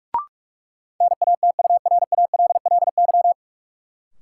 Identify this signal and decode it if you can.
Morse code, heard as a keyed tone, about 46 wpm, 700 Hz. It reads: NATURALLY
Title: Sentences from Top 1300 Words - New Word Drill Alt - 45wpm